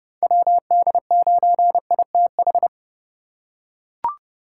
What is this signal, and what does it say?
Morse code, heard as a keyed tone, about 30 wpm, 700 Hz. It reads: WD9ITH
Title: USA Call Signs - 8x Character Spacing - 30wpm